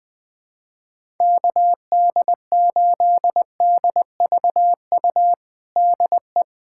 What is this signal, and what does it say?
Morse code, heard as a keyed tone, about 21 wpm, 700 Hz. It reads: KD8DVU DE
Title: USA Callsign DE Callsign 20wpm